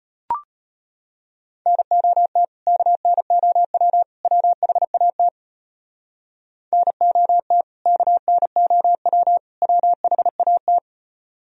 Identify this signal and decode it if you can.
Morse code, heard as a keyed tone, about 31 wpm, 700 Hz. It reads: NOT KNOW WHAT NOT KNOW WHAT